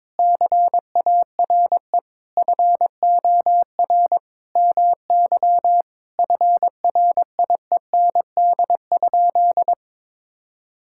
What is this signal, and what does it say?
Morse code, heard as a keyed tone, about 22 wpm, 700 Hz. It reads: CARE FOR MY FRIEND?
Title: Sentences from Top 1000 Words - Review Part 5 - 22wpm